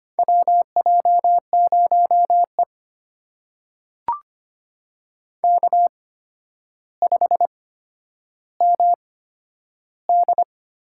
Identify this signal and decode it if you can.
Morse code, heard as a keyed tone, about 25 wpm, spaced as for 11 wpm, 700 Hz. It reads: WJ0E K5MD
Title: USA Call Signs - 4 Characters - 8x Character Spacing - 25wpm